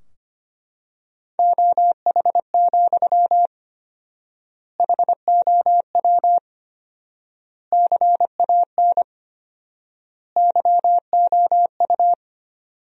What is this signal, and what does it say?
Morse code, heard as a keyed tone, about 25 wpm, 700 Hz. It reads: OH, HOW CAN YOU
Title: 4x Word-Spacing - Sentences from Top 500 Words - 25wpm